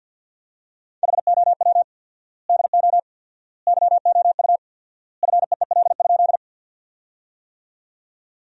Morse code, heard as a keyed tone, about 50 wpm, 700 Hz. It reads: HOW DO YOU FEEL?
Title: 4x Word-Spacing - Sentences from Top 400 Words - 50wpm